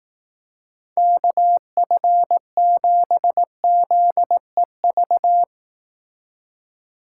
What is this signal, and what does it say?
Morse code, heard as a keyed tone, about 19 wpm, 700 Hz. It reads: KF7ZEV
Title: Speed Racing - USA 2x3 Call Signs 15wpm